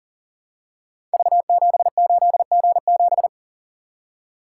Morse code, heard as a keyed tone, about 40 wpm, 700 Hz. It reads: V78G7
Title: Code Groups - 5 Letter:Numbers - 40wpm